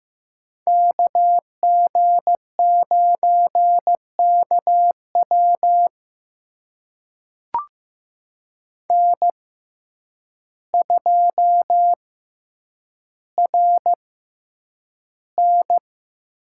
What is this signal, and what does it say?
Morse code, heard as a keyed tone, about 15 wpm, spaced as for 8 wpm, 700 Hz. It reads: KG9KW N2RN
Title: USA Call Signs - 6x Character Spacing - 15wpm